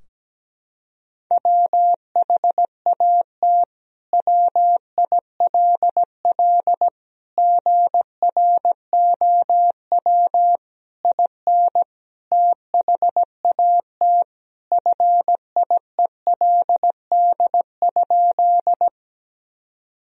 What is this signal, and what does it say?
Morse code, heard as a keyed tone, about 17 wpm, 700 Hz. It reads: WHAT WILL GROW IN THAT FIELD?
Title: Sentences from Top 1000 Words - Review Part 2 - 17wpm